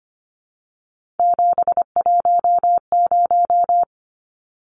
Morse code, heard as a keyed tone, about 25 wpm, 700 Hz. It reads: 710
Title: Numbers - 3-Digit 25wpm